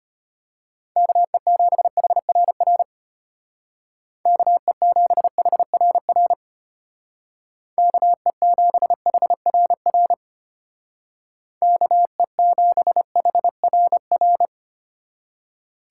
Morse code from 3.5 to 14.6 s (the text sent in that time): KE7HRR KE7HRR KE7HRR